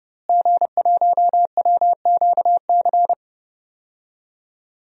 Morse code, heard as a keyed tone, about 32 wpm, 700 Hz. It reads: G1WQC
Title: DXCC-100 Call Signs - 5 Characters - Repeated 2x - 30wpm